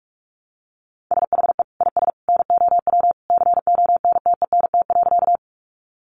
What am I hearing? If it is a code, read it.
Morse code, heard as a keyed tone, about 44 wpm, 700 Hz. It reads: SHE IS NOW CONTENT.
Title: Sentences from Top 900 Words 45wpm